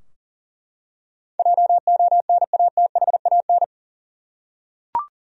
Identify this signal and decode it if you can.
Morse code, heard as a keyed tone, about 39 wpm, 700 Hz. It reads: JONATHAN
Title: USA Common Names - 40wpm